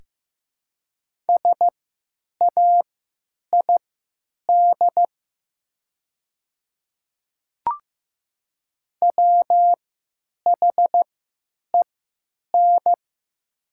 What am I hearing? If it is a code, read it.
Morse code, heard as a keyed tone, about 15 wpm, spaced as for 9 wpm, 700 Hz. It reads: SAID WHEN
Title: 3x Character Spacing - Top 100 Words - 15wpm